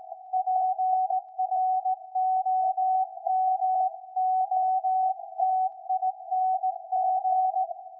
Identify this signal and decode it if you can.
Morse code, heard as a keyed tone, about 17 wpm, 750 Hz. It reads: PROMOTING
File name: iz7slz / VLF / SAQ 13-02-2023